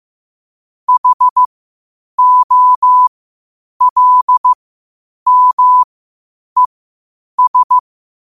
Morse code, heard as a keyed tone, about 15 wpm, spaced as for 8 wpm, 1000 Hz. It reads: HOLMES
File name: qotdfiles / 8wpm